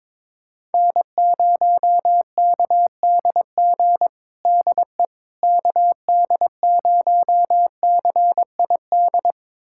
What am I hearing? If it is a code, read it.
Morse code, heard as a keyed tone, about 22 wpm, 700 Hz. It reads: N0KDG DE KD0CID
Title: USA Callsign DE Callsign 22wpm